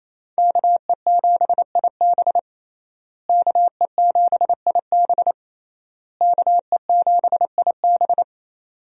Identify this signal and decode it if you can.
Morse code, heard as a keyed tone, about 28 wpm, 700 Hz. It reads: KE7IB KE7IB KE7IB
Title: USA Call Signs - 5 Characters - Repeated 3x - 28wpm